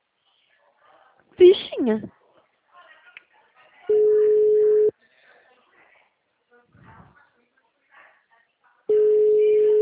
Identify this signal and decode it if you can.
Ringback tone (Europe, 425 Hz), with speech.